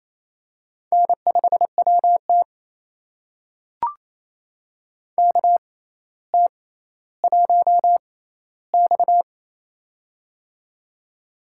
Morse code, heard as a keyed tone, about 28 wpm, spaced as for 14 wpm, 700 Hz. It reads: N5WT KT1X
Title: USA Call Signs - 4 Characters - 6x Character Spacing - 28wpm